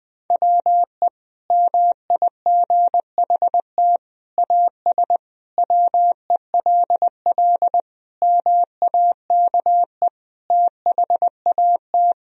Morse code, heard as a keyed tone, about 20 wpm, 700 Hz. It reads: WE MIGHT AS WELL MAKE THAT
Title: Sentences from Top 1000 Words - Review Part 1 - 20wpm